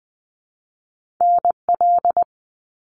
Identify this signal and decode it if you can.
Morse code, heard as a keyed tone, about 20 wpm, 700 Hz. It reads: NL
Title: Canadian Provinces and Territories - Repeated 3x - 20wpm